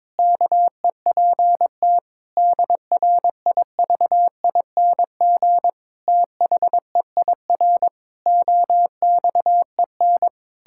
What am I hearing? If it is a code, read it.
Morse code, heard as a keyed tone, about 22 wpm, 700 Hz. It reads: KEPT DRIVING THEIR OXEN
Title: Pangram Challenge v2 - 22wpm